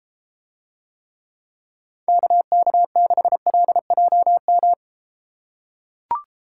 Morse code, heard as a keyed tone, about 33 wpm, 700 Hz. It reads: KK6LJM